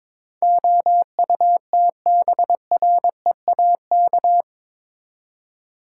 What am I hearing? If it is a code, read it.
Morse code, heard as a keyed tone, about 22 wpm, 700 Hz. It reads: OUTBREAK
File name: practice / 22wpm / Eight Letter Words - 22wpm